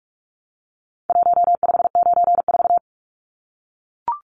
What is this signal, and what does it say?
Morse code, heard as a keyed tone, about 44 wpm, 700 Hz. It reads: 1594
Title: Numbers - Contest Style - 4-Digit 45wpm